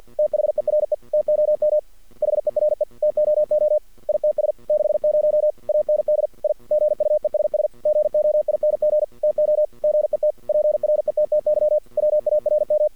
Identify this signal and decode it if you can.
Morse code, heard as a keyed tone, about 53 wpm, 610 Hz. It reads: THERE TOM HERE TOO ITS 60 AND I GUESS GOING TO GET PRETTY WARM
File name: archived / CRSnet